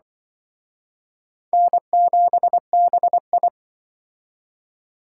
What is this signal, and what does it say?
Morse code, heard as a keyed tone, about 24 wpm, 700 Hz. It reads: N7BI
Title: Speed Racing - USA 1x2 Call Signs 20wpm